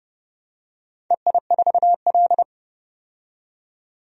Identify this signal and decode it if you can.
Morse code, heard as a keyed tone, about 30 wpm, 700 Hz. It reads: EI4L
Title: DXCC-100 Call Signs - 4 Characters - 30wpm